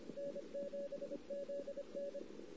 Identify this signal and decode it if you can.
Morse code, heard as a keyed tone, about 27 wpm, 550 Hz, with noise. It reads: N7ZN
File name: CW / 2008-ss-cw-k5tr-op-n5rz -16:00